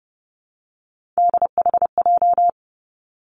Morse code, heard as a keyed tone, about 30 wpm, 700 Hz. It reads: DHJ
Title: Three Challenging Letter-Number Combinations 30wpm